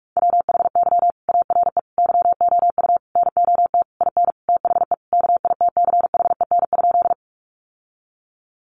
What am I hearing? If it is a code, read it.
Morse code, heard as a keyed tone, about 44 wpm, 700 Hz. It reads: WHY ARE YOU NOT IN THE KITCHEN?
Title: Sentences from Top 1400 Words 45wpm